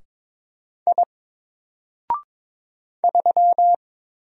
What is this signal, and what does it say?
Morse code, heard as a keyed tone, about 22 wpm, 700 Hz. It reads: I 3
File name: Single-Letter-Number-22wpm